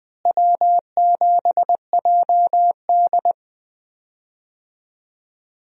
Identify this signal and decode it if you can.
Morse code, heard as a keyed tone, about 21 wpm, 700 Hz. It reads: W7JD
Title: USA Call Signs - 4 Characters - Repeated 3x - 20wpm